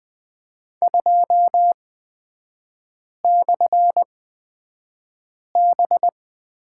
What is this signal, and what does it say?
Morse code, heard as a keyed tone, about 20 wpm, spaced as for 5 wpm, 700 Hz. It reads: 2/B
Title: 236 Lesson 38e - Letter q - Three Characters 20wpm